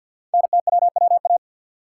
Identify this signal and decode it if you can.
Morse code, heard as a keyed tone, about 51 wpm, 690 Hz. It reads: NTWWA
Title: Code Groups - 5 Letters - 50wpm